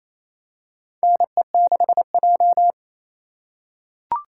Morse code, heard as a keyed tone, about 28 wpm, 700 Hz. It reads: NE6J
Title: USA Call Signs - 4 Characters - 4x Character Spacing - 28wpm